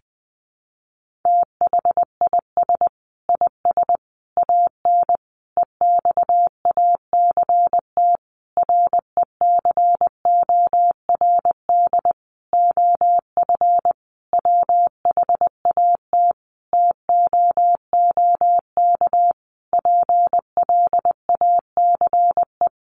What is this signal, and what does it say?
Morse code, heard as a keyed tone, about 20 wpm, 700 Hz. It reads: THIS IS AN EXACT RECORD OF WHAT TOOK PLACE
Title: Sentences from Top 1300 Words 20wpm